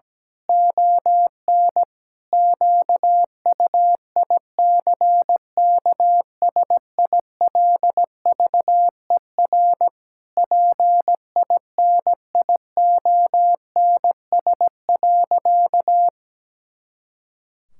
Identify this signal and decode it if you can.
Morse code, heard as a keyed tone, about 17 wpm, 700 Hz. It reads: ON QUICKSILVER PINIONS.
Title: Pangram Challenge v2 - 17wpm